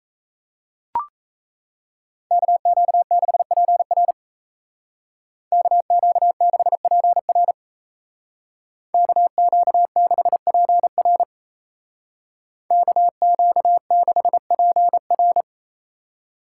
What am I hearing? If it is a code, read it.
Morse code, heard as a keyed tone, about 36 wpm, 700 Hz. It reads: KQ6PR KQ6PR KQ6PR KQ6PR